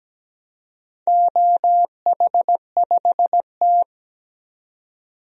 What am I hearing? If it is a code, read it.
Morse code, heard as a keyed tone, about 17 wpm, 700 Hz. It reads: OH5T
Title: DXCC-100 Call Signs - 4 Characters - 17wpm